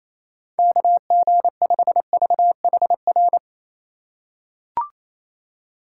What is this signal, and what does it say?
Morse code, heard as a keyed tone, about 28 wpm, 700 Hz. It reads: KG5VHR
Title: USA Call Signs - 6 Characters - 3x Character Spacing - 28wpm